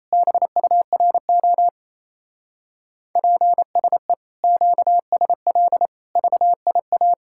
Morse code, heard as a keyed tone, about 32 wpm, 700 Hz. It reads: BURO PSE QSL VIA
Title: Speed-Racing Common QSO Phrases - 28wpm